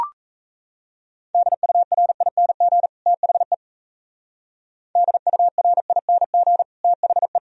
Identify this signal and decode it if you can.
Morse code, heard as a keyed tone, about 37 wpm, 700 Hz. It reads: DURING THE DURING THE